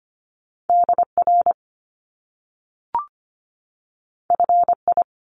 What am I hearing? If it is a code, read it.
Morse code, heard as a keyed tone, about 25 wpm, 700 Hz. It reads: DR FI
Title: Top Two Letter Combinations in Words 25wpm